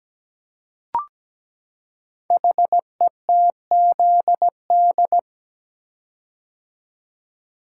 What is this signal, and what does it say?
Morse code, heard as a keyed tone, about 17 wpm, 700 Hz. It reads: HETZD
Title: Code Groups - 5 Letters - 17wpm